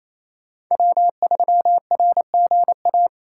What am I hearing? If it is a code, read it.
Morse code, heard as a keyed tone, about 28 wpm, 700 Hz. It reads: W3RGA
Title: K1USN SST QSO Practice - 28wpm